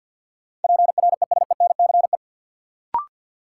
Morse code, heard as a keyed tone, about 50 wpm, 700 Hz. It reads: PRESENCE
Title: Sets of 3 Words - New Word Drill - 50wpm